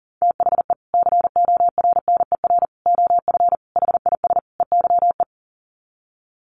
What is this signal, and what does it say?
Morse code, heard as a keyed tone, about 37 wpm, 700 Hz. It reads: THE CORNER OF HIS EYE
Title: Sets of 5 Words Encore2 40wpm